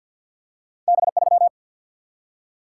Morse code, heard as a keyed tone, about 54 wpm, 700 Hz. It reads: B 3
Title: Single-Letter Single-Number 50wpm